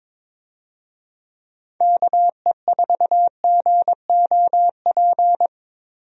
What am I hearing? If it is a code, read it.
Morse code, heard as a keyed tone, about 22 wpm, 700 Hz. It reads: KE4GOP
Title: USA Call Signs - 6 Characters - Repeated 3x - 22wpm